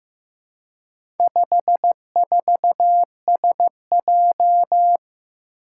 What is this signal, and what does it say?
Morse code, heard as a keyed tone, about 15 wpm, 700 Hz. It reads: 54SJ
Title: Four Challenging Letter-Number Combinations 15wpm